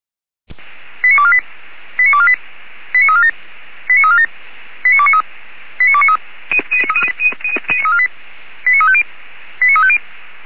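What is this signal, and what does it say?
Five-tone selective call (ZVEI), heard as quick runs of five tones, tones of about 70 ms. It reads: 89248 89248 89347 89347 89293 89293 89350 89350
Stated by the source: ZVEI five-tone selective call 89332, received with an SDR.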